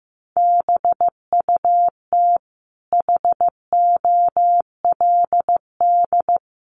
Morse code, heard as a keyed tone, about 15 wpm, 700 Hz. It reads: BUT HOLD
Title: Sentences from Top 1300 Words 15wpm